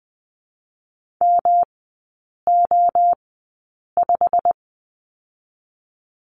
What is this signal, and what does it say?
Morse code, heard as a keyed tone, about 20 wpm, spaced as for 8 wpm, 700 Hz. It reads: MO5
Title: 107 Lesson 18e - Letter m - Three Characters 20wpm